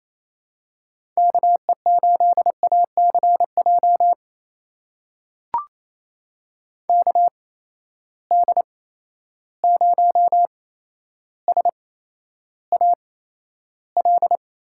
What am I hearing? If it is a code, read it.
Morse code, heard as a keyed tone, about 28 wpm, spaced as for 12 wpm, 700 Hz. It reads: KE8ACJ KD0SAL
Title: USA Call Signs - 6 Characters - 8x Character Spacing - 28wpm